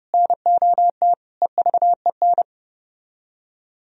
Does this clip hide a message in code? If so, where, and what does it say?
Yes: Morse code from 0.1 to 2.4 s, heard as a keyed tone, about 33 wpm, 700 Hz. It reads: NOT EVEN